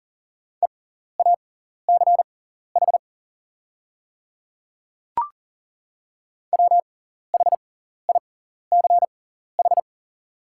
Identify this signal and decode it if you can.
Morse code, heard as a keyed tone, about 40 wpm, spaced as for 14 wpm, 700 Hz. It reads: EACH WHICH
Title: 6x Character Spacing - Top 100 Words - 40wpm